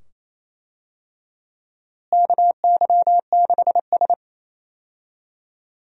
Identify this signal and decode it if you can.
Morse code, heard as a keyed tone, about 28 wpm, 700 Hz. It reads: KY6S
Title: USA Call Signs - 4 Characters - Repeated 2x - 28wpm